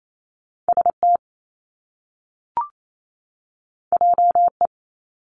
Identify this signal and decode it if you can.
Morse code, heard as a keyed tone, about 28 wpm, 700 Hz. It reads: ST JE
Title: Top Two Letter Combinations in Words 28wpm